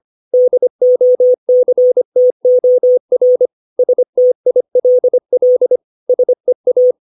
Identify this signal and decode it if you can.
Morse code, heard as a keyed tone, about 25 wpm, 500 Hz. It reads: DOCTOR STILL SEA
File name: 632 words / 632 Words 25WPM